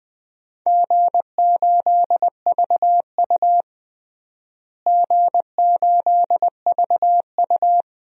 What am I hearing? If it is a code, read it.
Morse code, heard as a keyed tone, about 20 wpm, 700 Hz. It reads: G8VU G8VU
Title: DXCC-100 Call Signs - 4 Characters - Repeated 2x - 20wpm